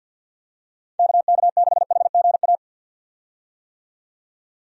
Morse code, heard as a keyed tone, about 49 wpm, 700 Hz. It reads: KK6HGA
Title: USA Call Signs - 6 Characters - 50wpm